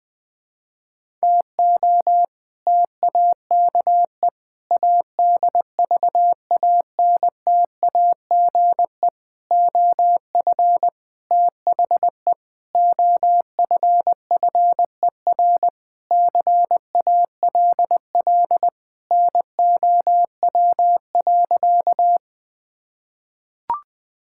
Morse code, heard as a keyed tone, about 20 wpm, 700 Hz. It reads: TO TAKE ADVANTAGE OF THE OFFER CALL NOW.
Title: Sentences with 4-Word Phrases - 20wpm